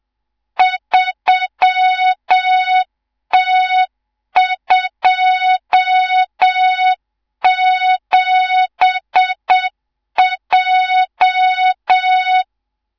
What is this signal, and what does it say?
Morse code, heard as a keyed tone, about 7 wpm, 3700 Hz, with noise. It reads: 3T27J
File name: Morze-HA2MN / hang / 18 IRASJELEK2